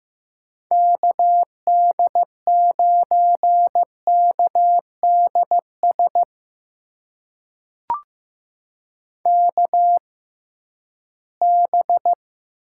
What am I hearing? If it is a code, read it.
Morse code, heard as a keyed tone, about 15 wpm, spaced as for 11 wpm, 700 Hz. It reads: KD9KDS KB
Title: USA Call Signs - 6 Characters - 6x Character Spacing - 15wpm